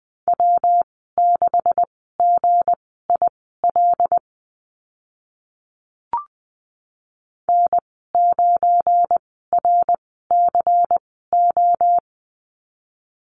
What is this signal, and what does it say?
Morse code, heard as a keyed tone, about 20 wpm, spaced as for 15 wpm, 700 Hz. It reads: W6GIL N9RCO
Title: 186 Lesson 30g - Letter b - Call Signs 20wpm